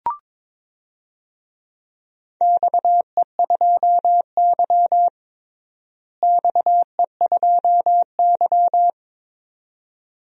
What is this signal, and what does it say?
Morse code, heard as a keyed tone, about 22 wpm, 700 Hz. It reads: XE2Y XE2Y
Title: DXCC-100 Call Signs - 4 Characters - Repeated 2x - 22wpm